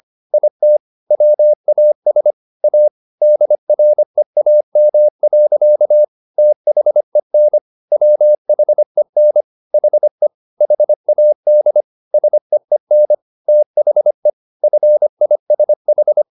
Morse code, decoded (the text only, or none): IT WAS A DREAM. THEN WHEN HE HAD SEEN THE FISH